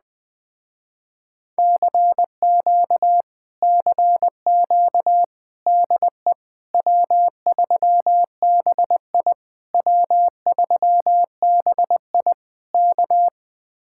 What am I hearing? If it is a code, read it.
Morse code, heard as a keyed tone, about 20 wpm, 700 Hz. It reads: CQ CQ DE W3BI W3BI K
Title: Speed-Racing Common QSO Phrases Encore - 17wpm